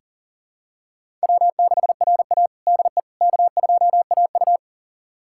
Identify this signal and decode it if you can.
Morse code, heard as a keyed tone, about 37 wpm, 700 Hz. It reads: W6RA DE K2AU
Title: USA Callsign DE Callsign - 4 Characters - 40wpm